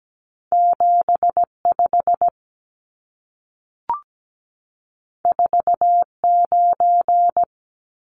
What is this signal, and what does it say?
Morse code, heard as a keyed tone, about 17 wpm, 700 Hz. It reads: 75 49
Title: Numbers - 2-Digit 17wpm